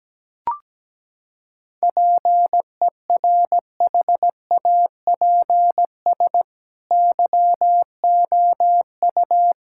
Morse code, heard as a keyed tone, about 17 wpm, 700 Hz. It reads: PERHAPS YOU
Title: Sentences from Top 1000 Words - Review Part 6 - 17wpm